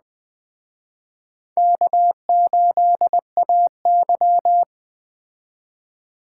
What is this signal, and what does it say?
Morse code, heard as a keyed tone, about 21 wpm, 700 Hz. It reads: K8AY
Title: Speed Racing - USA 1x2 Call Signs 17wpm